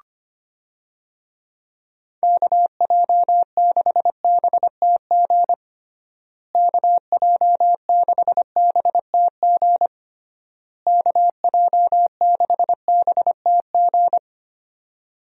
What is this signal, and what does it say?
Morse code, heard as a keyed tone, about 25 wpm, 700 Hz. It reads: KJ6BTG KJ6BTG KJ6BTG
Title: USA Call Signs - 6 Characters - Repeated 3x - 25wpm